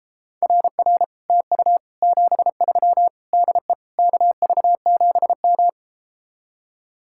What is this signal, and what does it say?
Morse code, heard as a keyed tone, about 33 wpm, 700 Hz. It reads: RR TU 73 DE KV7M